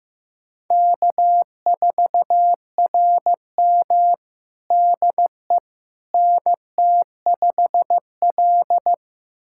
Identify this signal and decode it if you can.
Morse code, heard as a keyed tone, about 15 wpm, 700 Hz. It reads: K4RM DE NT5L
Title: USA Callsign DE Callsign - 4 Characters - 15wpm